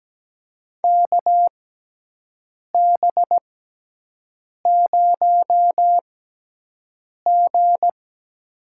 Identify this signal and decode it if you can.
Morse code, heard as a keyed tone, about 17 wpm, spaced as for 6 wpm, 700 Hz. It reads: KB0G
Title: USA Call Signs - 6 Characters - 6x Character Spacing - 17wpm